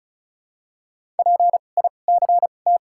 Morse code, heard as a keyed tone, about 36 wpm, spaced as for 25 wpm, 700 Hz. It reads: PICT
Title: 2x Character Spacing - Top 500 Words - 35wpm